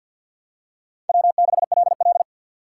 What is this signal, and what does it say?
Morse code, heard as a keyed tone, about 50 wpm, 700 Hz. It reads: W6LL
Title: USA Call Signs - 4 Characters - 3x Character Spacing - 50wpm